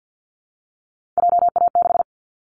Morse code, heard as a keyed tone, about 50 wpm, 700 Hz. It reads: JA6
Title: Callsign Prefixes for Top 25 DXCC - 50wpm